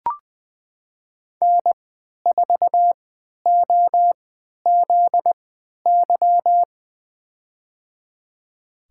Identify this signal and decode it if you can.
Morse code, heard as a keyed tone, about 21 wpm, spaced as for 12 wpm, 700 Hz. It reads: N4OZY
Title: USA Call Signs - 5 Characters - 3x Character Spacing - 20wpm